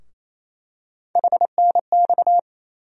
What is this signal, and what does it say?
Morse code, heard as a keyed tone, about 28 wpm, 700 Hz. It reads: HNX
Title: USA Call Sign Postfixes - 3 Letters 28wpm